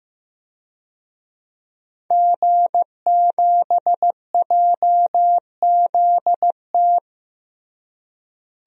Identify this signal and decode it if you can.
Morse code, heard as a keyed tone, about 15 wpm, 700 Hz. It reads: G7JZT
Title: DXCC-100 Call Signs - 5 Characters - Repeated 2x - 15wpm